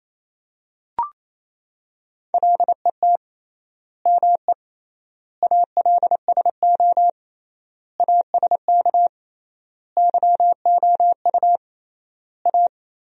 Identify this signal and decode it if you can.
Morse code, heard as a keyed tone, about 28 wpm, 700 Hz. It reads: LET ME ALSO ASK YOU A
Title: 3x Word-Spacing - Sentences from Top 400 Words - 28wpm